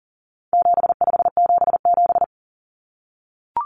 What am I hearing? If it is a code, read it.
Morse code, heard as a keyed tone, about 40 wpm, 700 Hz. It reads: 7577
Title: Common Cut Numbers - 4-Digit 40wpm